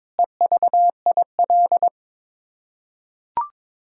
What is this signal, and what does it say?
Morse code, heard as a keyed tone, about 22 wpm, 700 Hz. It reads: EVIL
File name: Sets of 9 Words - 22wpm